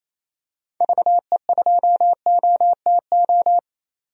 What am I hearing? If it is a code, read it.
Morse code, heard as a keyed tone, about 28 wpm, 700 Hz. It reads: VE2OTO
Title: DXCC-100 Call Signs - 6 Characters - 28wpm